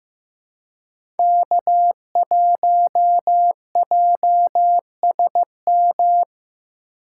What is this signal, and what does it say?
Morse code, heard as a keyed tone, about 15 wpm, 700 Hz. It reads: K1JSM
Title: USA Call Signs - 5 Characters - 2x Character Spacing - 15wpm